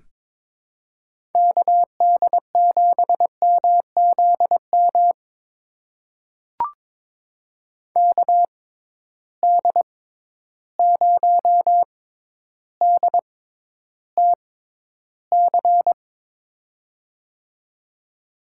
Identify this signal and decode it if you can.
Morse code, heard as a keyed tone, about 21 wpm, spaced as for 11 wpm, 700 Hz. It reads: KD7MZM KD0DTC
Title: USA Call Signs - 6 Characters - 6x Character Spacing - 22wpm